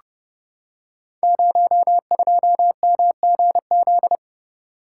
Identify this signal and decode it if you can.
Morse code, heard as a keyed tone, about 30 wpm, 700 Hz. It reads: 02MGZ
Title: Code Groups - 5 Letter:Numbers - 30wpm